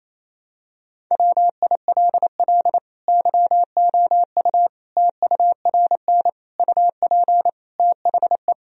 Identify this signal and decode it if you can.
Morse code, heard as a keyed tone, about 28 wpm, 700 Hz. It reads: WILL YOU TURN UP THE
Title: Sentences from Top 1000 Words - Review Part 3 - 28wpm